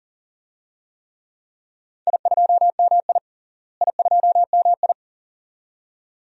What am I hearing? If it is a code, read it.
Morse code, heard as a keyed tone, about 40 wpm, 700 Hz. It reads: I2MI I2MI
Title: DXCC-100 Call Signs - 4 Characters - Repeated 2x - 40wpm